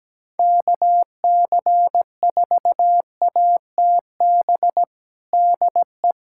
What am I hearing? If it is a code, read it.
Morse code, heard as a keyed tone, about 17 wpm, 700 Hz. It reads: KC4ATB DE
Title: USA Callsign DE Callsign - 6 Characters - 17wpm